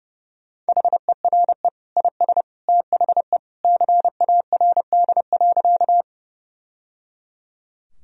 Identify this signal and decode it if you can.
Morse code, heard as a keyed tone, about 29 wpm, 700 Hz. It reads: HERE IS THE CARD.
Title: Sentences from Top 1000 Words - Review Part 4 - 30wpm